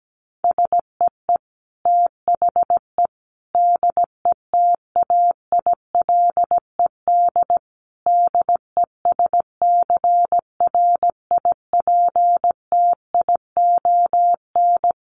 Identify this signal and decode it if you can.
Morse code, heard as a keyed tone, about 17 wpm, 700 Hz. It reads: SEE THE DETAILED DESCRIPTION
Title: Sentences from Top 1500 Words 17wpm